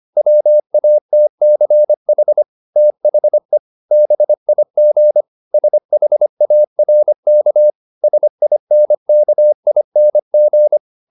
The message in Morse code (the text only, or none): WATCH THE BIG SHARK SINKING